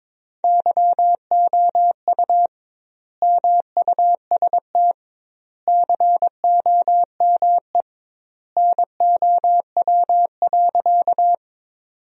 Morse code, heard as a keyed tone, about 22 wpm, 700 Hz. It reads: YOU MUST COME NOW.
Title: 2x Word-Spacing - Sentences from Top 200 Words - 22wpm